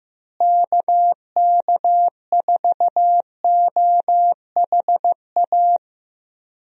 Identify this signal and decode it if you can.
Morse code, heard as a keyed tone, about 15 wpm, 700 Hz. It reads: KK4OHA
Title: USA Call Signs - 6 Characters - Repeated 3x - 15wpm